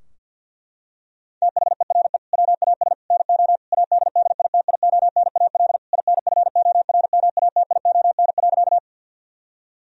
Morse code, heard as a keyed tone, about 48 wpm, 700 Hz. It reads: THERE WAS NO ADDITIONAL INFORMATION.
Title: Sentences from Top 1000 Words - Review Part 4 - 50wpm